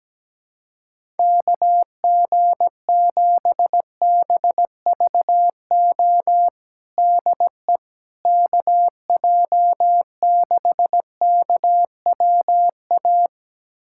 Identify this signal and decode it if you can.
Morse code, heard as a keyed tone, about 17 wpm, 700 Hz. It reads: KG7BVO DE KJ6KWA
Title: USA Callsign DE Callsign - 6 Characters - 17wpm